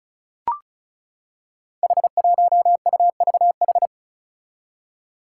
Morse code, heard as a keyed tone, about 36 wpm, 700 Hz. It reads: H1UVH
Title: Five Challenging Letter-Number Combinations 35wpm